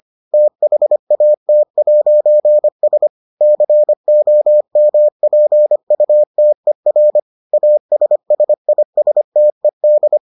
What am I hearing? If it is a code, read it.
Morse code, heard as a keyed tone, about 25 wpm, 600 Hz. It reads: THAT'S COMPUTER ASSISTED